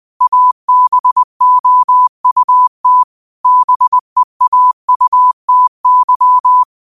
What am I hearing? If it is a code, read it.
Morse code, heard as a keyed tone, about 20 wpm, 1000 Hz. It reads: ABOUT BEAUTY